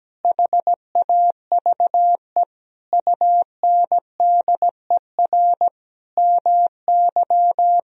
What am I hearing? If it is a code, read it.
Morse code, heard as a keyed tone, about 17 wpm, 700 Hz. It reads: HAVE UNDER MY
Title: Sentences from Top 1000 Words - Review Part 1 - 17wpm